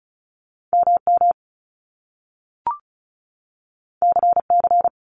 Morse code, heard as a keyed tone, about 36 wpm, 700 Hz. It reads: MM CC
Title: Top Two Letter Combinations in Words 35wpm